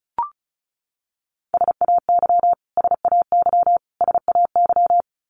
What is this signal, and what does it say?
Morse code, heard as a keyed tone, about 34 wpm, 700 Hz. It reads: SAY SAY SAY
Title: Top 200 Words Repeated 35wpm